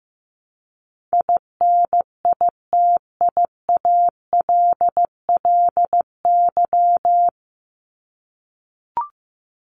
Morse code, heard as a keyed tone, about 15 wpm, 700 Hz. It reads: INITIALLY
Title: Top 1251-1500 Words 15wpm